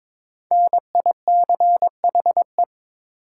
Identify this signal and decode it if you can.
Morse code, heard as a keyed tone, about 22 wpm, 690 Hz. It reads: NICHE
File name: 5 Letter Words - First 3 Letters Given - 22wpm